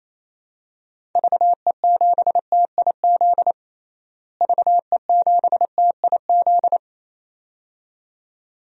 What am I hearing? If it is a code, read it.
Morse code, heard as a keyed tone, about 27 wpm, 700 Hz. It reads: VE7TIZ VE7TIZ
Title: DXCC-100 Call Signs - 6 Characters - Repeated 2x - 28wpm